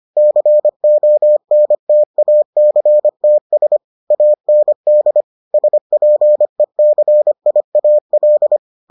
Morse code, heard as a keyed tone, about 25 wpm, 600 Hz. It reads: CONTACTS AND SPECIAL